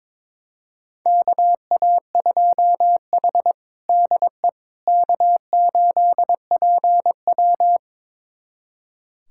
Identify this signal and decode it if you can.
Morse code, heard as a keyed tone, about 21 wpm, 700 Hz. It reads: KA2H DE K8PW
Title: USA Callsign DE Callsign - 4 Characters - 22wpm